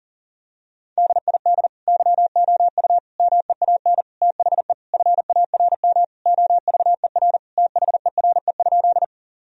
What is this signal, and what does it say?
Morse code, heard as a keyed tone, about 37 wpm, 700 Hz. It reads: DID YOU MEAN THE FARM OVER THERE?
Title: Sentences from Top 1000 Words - Review Part 2 - 40wpm